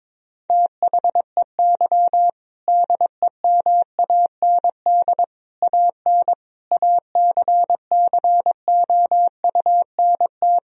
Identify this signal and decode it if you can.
Morse code, heard as a keyed tone, about 22 wpm, 700 Hz. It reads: THEY DEMAND AN ACCOUNT
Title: Sentences from Top 1000 Words - Review Part 4 - 22wpm